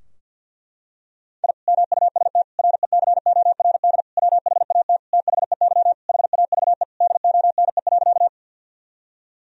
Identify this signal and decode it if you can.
Morse code, heard as a keyed tone, about 52 wpm, 700 Hz. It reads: I MUST RECORD WHAT THEY HAVE DONE.
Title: Sentences from Top 1000 Words - Review Part 3 - 50wpm